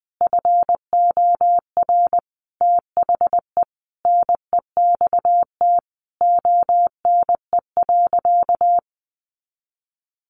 Morse code, heard as a keyed tone, about 20 wpm, 700 Hz. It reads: FOR THE NEXT ONE.
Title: Sentences from Top 400 Words 20wpm